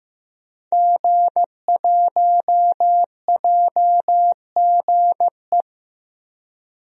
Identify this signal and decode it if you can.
Morse code, heard as a keyed tone, about 15 wpm, 700 Hz. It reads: G1JGE
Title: DXCC-100 Call Signs - 5 Characters - Repeated 3x - 15wpm